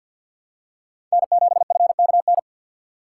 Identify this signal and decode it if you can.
Morse code, heard as a keyed tone, about 46 wpm, 700 Hz. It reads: N7FKN
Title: USA Call Signs - 5 Characters - 3x Character Spacing - 50wpm